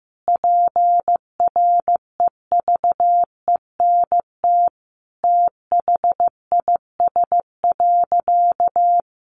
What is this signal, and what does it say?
Morse code, heard as a keyed tone, about 15 wpm, 700 Hz. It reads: PREVENT THIS.
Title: Sentences from Top 1100 Words 15wpm